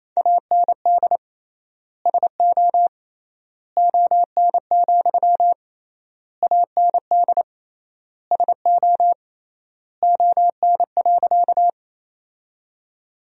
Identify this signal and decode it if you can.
Morse code, heard as a keyed tone, about 28 wpm, 700 Hz. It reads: AND SO ON, AND SO ON.
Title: 3x Word-Spacing - Sentences from Top 100 Words - 28wpm